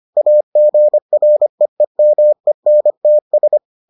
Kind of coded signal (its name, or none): Morse code